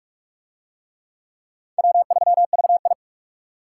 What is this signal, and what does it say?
Morse code, heard as a keyed tone, about 40 wpm, 700 Hz. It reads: W3VI